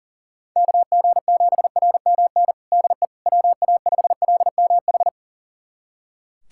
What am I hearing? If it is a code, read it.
Morse code, heard as a keyed tone, about 37 wpm, 700 Hz. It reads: KG7RMN DE WA5LMH
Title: USA Callsign DE Callsign - 6 Characters - 40wpm